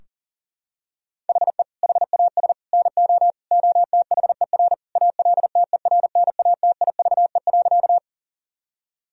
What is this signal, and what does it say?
Morse code, heard as a keyed tone, about 40 wpm, 700 Hz. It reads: HE HAS NO OTHER ALTERNATIVE.
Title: Sentences from Top 1000 Words - Review Part 4 - 40wpm